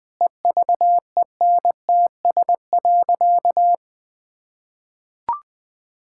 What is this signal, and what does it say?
Morse code, heard as a keyed tone, about 20 wpm, 700 Hz. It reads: EVENTS.
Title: Sentences with 8-Word Phrases - 20wpm